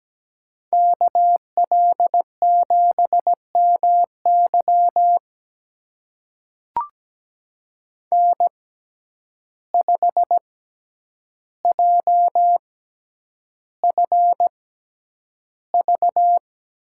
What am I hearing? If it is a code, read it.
Morse code, heard as a keyed tone, about 17 wpm, spaced as for 9 wpm, 700 Hz. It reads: KL7MY N5JFV
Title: USA Call Signs - 5 Characters - 6x Character Spacing - 17wpm